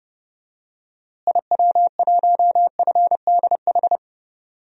Morse code, heard as a keyed tone, about 30 wpm, 700 Hz. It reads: IW1FDH
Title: DXCC-100 Call Signs - 6 Characters - Repeated 3x - 30wpm